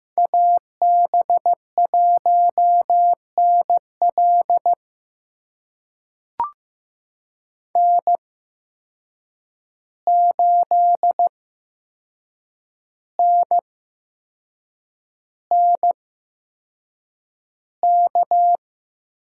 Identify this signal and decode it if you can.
Morse code, heard as a keyed tone, about 15 wpm, spaced as for 6 wpm, 700 Hz. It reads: AB1NL N8NNK
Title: USA Call Signs - 5 Characters - 8x Character Spacing - 15wpm